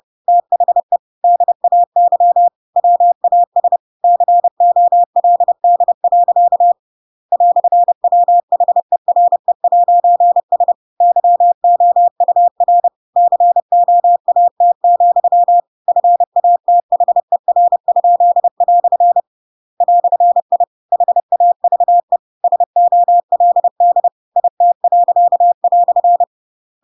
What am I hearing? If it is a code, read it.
Morse code, heard as a keyed tone, about 30 wpm, 700 Hz. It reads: THE DAY WAS COLD. "WHERE'S YOUR COAT, FATHER?" "I HAVE SOLD IT."
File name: Pinocchio / Pinocchio ch8 30 30 ik1hge